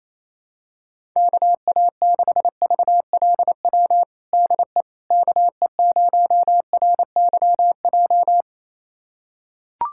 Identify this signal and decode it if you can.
Morse code, heard as a keyed tone, about 28 wpm, 700 Hz. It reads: KA6VLW DE KE0RYJ
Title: USA Callsign DE Callsign - 6 Characters - 28wpm